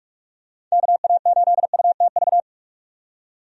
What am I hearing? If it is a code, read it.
Morse code, heard as a keyed tone, about 44 wpm, 700 Hz. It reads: KA8UTV
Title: USA Call Signs - 6 Characters - 45wpm